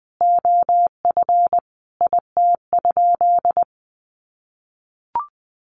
Morse code, heard as a keyed tone, about 19 wpm, 700 Hz. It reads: OF IT?
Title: Sentences from Top 1100 Words 20wpm